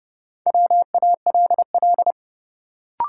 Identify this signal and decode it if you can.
Morse code, heard as a keyed tone, about 32 wpm, 700 Hz. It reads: WALL
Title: Sentences from Top 1100 Words - New Word Drill Alt - 30wpm